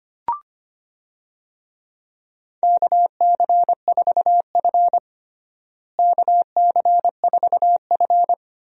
Morse code, heard as a keyed tone, about 25 wpm, 700 Hz. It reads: KC4F KC4F
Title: USA Call Signs - 4 Characters - Repeated 2x - 25wpm